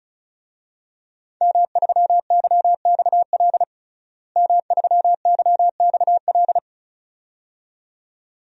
Morse code, heard as a keyed tone, about 36 wpm, 700 Hz. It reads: M3YXL M3YXL
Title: DXCC-100 Call Signs - 5 Characters - Repeated 2x - 35wpm